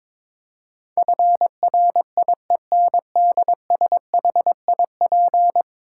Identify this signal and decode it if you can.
Morse code, heard as a keyed tone, about 22 wpm, 700 Hz. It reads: FRIENDSHIP
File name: practice / 22wpm / 10 Letter Words - First 3 Letters Given - 22wpm